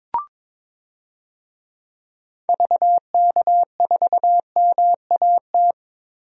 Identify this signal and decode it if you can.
Morse code, heard as a keyed tone, about 22 wpm, 700 Hz. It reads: VK4MAT
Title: DXCC-100 Call Signs - 6 Characters - 22wpm